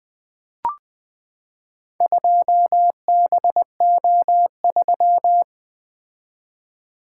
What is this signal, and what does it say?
Morse code, heard as a keyed tone, about 20 wpm, 700 Hz. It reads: 2BO3
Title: Four Letter-Number Combinations - 20wpm